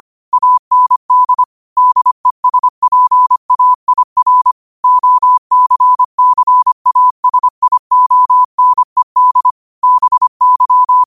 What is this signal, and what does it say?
Morse code, heard as a keyed tone, about 25 wpm, 1000 Hz. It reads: AND DESPAIR OCCASIONED BY
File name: qotdfiles / 25wpm